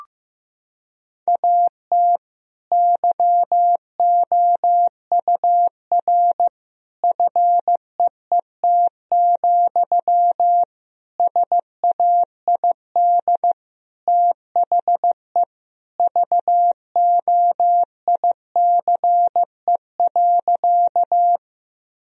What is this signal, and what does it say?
Morse code, heard as a keyed tone, about 15 wpm, 700 Hz. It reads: AT YOUR FEET, SAID THE VOICE.
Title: Sentences from Top 1000 Words - Review Part 1 - 15wpm